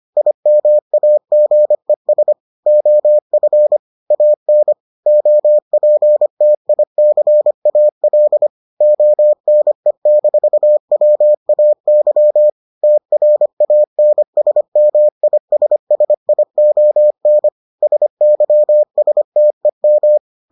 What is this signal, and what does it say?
Morse code, heard as a keyed tone, about 25 wpm, 600 Hz. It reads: IMAGES OF AN OPTICAL ONE-WAY TRANSMISSION SYSTEM